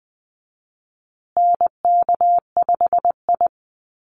Morse code, heard as a keyed tone, about 20 wpm, 700 Hz. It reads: NK5I
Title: USA Call Signs Repeated 2x 20wpm